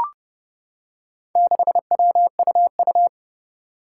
Morse code, heard as a keyed tone, about 28 wpm, 700 Hz. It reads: 6WUU
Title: Four Challenging Letter-Number Combinations 30wpm